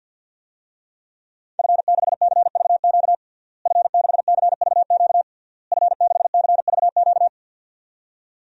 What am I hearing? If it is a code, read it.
Morse code, heard as a keyed tone, about 46 wpm, 700 Hz. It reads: F6CVQ F6CVQ F6CVQ
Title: DXCC-100 Call Signs - 5 Characters - Repeated 3x - 50wpm